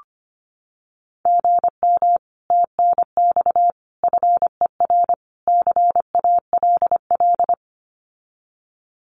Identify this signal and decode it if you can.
Morse code, heard as a keyed tone, about 25 wpm, 700 Hz. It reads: GM TNX FER CALL
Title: Common QSO Phrases Encore 25wpm